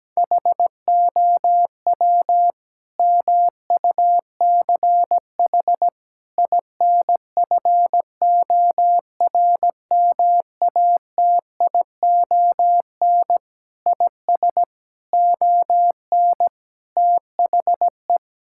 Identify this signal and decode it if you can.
Morse code, heard as a keyed tone, about 17 wpm, 700 Hz. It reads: HOW MUCH INFORMATION IS ON THE
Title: Sentences from Top 1000 Words - Review Part 6 - 17wpm